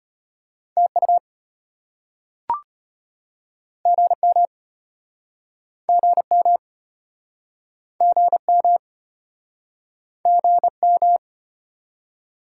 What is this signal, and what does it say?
Morse code, heard as a keyed tone, about 34 wpm, 700 Hz. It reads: TU GM GM GM GM